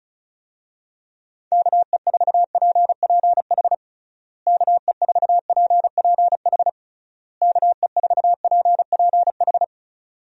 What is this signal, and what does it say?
Morse code, heard as a keyed tone, about 36 wpm, 700 Hz. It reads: KE4PPH KE4PPH KE4PPH
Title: USA Call Signs - 6 Characters - Repeated 3x - 35wpm